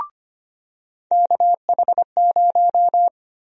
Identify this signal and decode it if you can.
Morse code, heard as a keyed tone, about 25 wpm, 700 Hz. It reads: KH0
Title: USA Call Sign Prefixes - 25wpm